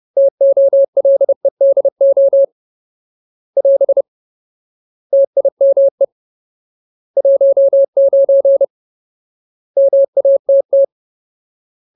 Morse code, heard as a keyed tone, about 30 wpm, 550 Hz. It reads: TOLEDO <AS> TIME 19 MATT
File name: Ultimate Word List / 30WPM Ultimate